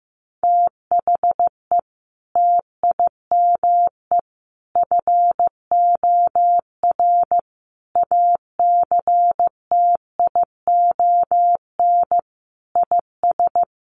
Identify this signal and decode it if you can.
Morse code, heard as a keyed tone, about 15 wpm, 700 Hz. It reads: THE TIME FOR ACTION IS
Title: Sentences from Top 700 Words 15wpm